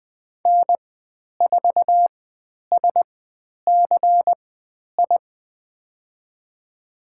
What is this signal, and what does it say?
Morse code, heard as a keyed tone, about 20 wpm, spaced as for 10 wpm, 700 Hz. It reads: N4SCI
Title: 102 Lesson 17g - Letter u - Call Signs 20wpm